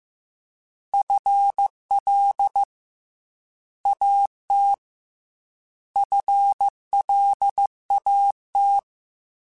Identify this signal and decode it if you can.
Morse code, heard as a keyed tone, about 15 wpm, 800 Hz. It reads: FL AT FLAT
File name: FL / FL mp3s / Fast FL @13